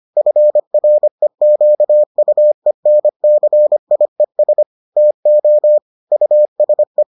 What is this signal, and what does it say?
Morse code, heard as a keyed tone, about 25 wpm, 600 Hz. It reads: FREQUENCIES TO USE